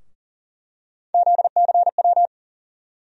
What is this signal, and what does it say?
Morse code, heard as a keyed tone, about 40 wpm, 700 Hz. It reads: ZCW
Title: USA Call Sign Postfixes - 3 Letters 40wpm